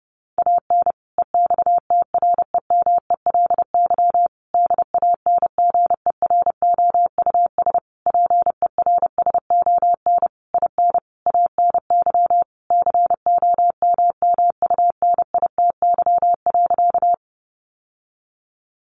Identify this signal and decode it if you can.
Morse code, heard as a keyed tone, about 30 wpm, 700 Hz. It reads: AN EXTREMELY DANGEROUS PERSON IN ANY COMMUNITY.
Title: Sentences from Top 1200 Words 30wpm